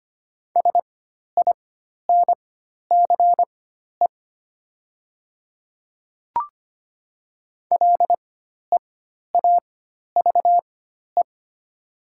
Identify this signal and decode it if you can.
Morse code, heard as a keyed tone, about 25 wpm, spaced as for 12 wpm, 700 Hz. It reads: SINCE LEAVE
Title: 4x Character Spacing - Top 500 Words - 25wpm